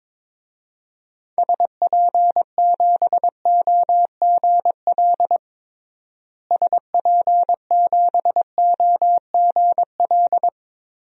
Morse code, heard as a keyed tone, about 22 wpm, 700 Hz. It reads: SP7OGL SP7OGL
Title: DXCC-100 Call Signs - 6 Characters - Repeated 2x - 22wpm